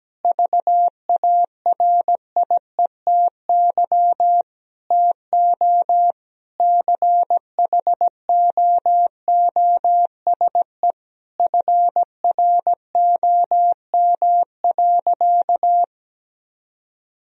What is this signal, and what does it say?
Morse code, heard as a keyed tone, about 17 wpm, 700 Hz. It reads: VARIETY TO CHOOSE FROM.
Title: Sentences from Top 1000 Words - Review Part 4 - 17wpm